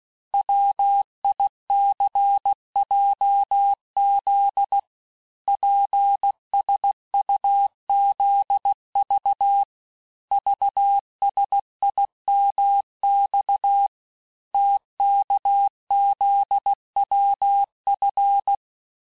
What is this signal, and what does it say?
Morse code, heard as a keyed tone, about 16 wpm, 800 Hz. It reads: WICJZ PSUZV VSIMX TKZWF
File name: cours-du-lundi files / Cours-1586-texte-chiffre-vitesse-15mpm